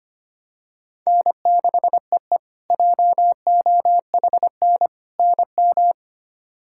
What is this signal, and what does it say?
Morse code, heard as a keyed tone, about 25 wpm, 700 Hz. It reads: N6EE JOHN NM
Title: K1USN SST QSO Practice - 25wpm